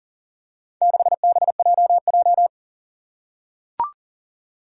Morse code, heard as a keyed tone, about 40 wpm, 700 Hz. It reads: 6BJJ
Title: Four Challenging Letter-Number Combinations 40wpm